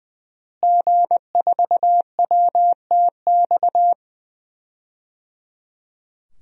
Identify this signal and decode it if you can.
Morse code, heard as a keyed tone, about 20 wpm, 700 Hz. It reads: G4WTX
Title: DXCC-100 Call Signs - 5 Characters - Repeated 3x - 20wpm